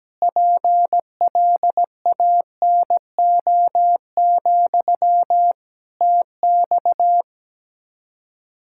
Morse code, heard as a keyed tone, about 17 wpm, 700 Hz. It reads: PLANO, TX
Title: K1USN SST QSO Practice 17wpm